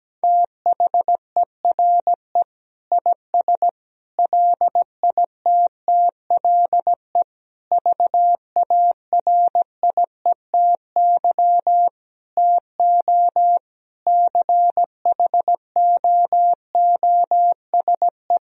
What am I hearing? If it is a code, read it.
Morse code, heard as a keyed tone, about 17 wpm, 700 Hz. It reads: THERE IS LITTLE VARIETY TO CHOOSE
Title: Sentences from Top 1000 Words - Review Part 4 - 17wpm